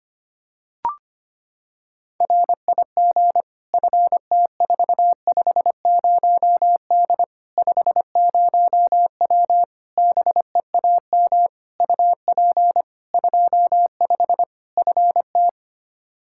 Morse code, heard as a keyed tone, about 25 wpm, 700 Hz. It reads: RIG FT450D 50W BEAM UP 25 FT
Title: Practice QSO - Exchange 2 - Info - 25wpm